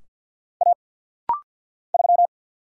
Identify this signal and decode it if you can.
Morse code, heard as a keyed tone, about 50 wpm, 700 Hz. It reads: A 3
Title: Single Letter-Number Alt - Rapid-Fire 50wpm